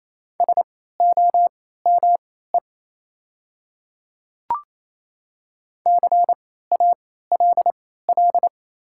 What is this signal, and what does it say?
Morse code, heard as a keyed tone, about 28 wpm, spaced as for 16 wpm, 700 Hz. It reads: SOME CALL
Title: 3x Character Spacing - Top 200 Words - 28wpm